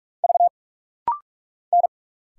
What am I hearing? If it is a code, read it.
Morse code, heard as a keyed tone, about 46 wpm, 700 Hz. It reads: V N
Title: Single Letter-Number - Rapid-Fire 45wpm